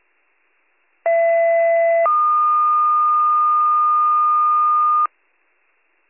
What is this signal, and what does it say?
Two-tone page, heard as a one-second tone, then a three-second tone, about 670 then 1220 Hz. It is heard from 1.1 to 5.1 s.